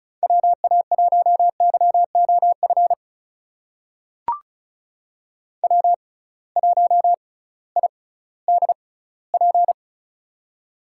Morse code, heard as a keyed tone, about 34 wpm, spaced as for 19 wpm, 700 Hz. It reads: WA1YOF W1IDP
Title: USA Call Signs - 6x Character Spacing - 35wpm